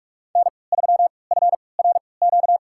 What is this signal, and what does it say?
Morse code, heard as a keyed tone, about 47 wpm, spaced as for 26 wpm, 700 Hz. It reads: N3FRQ
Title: USA Call Signs - 5 Characters - 3x Character Spacing - 45wpm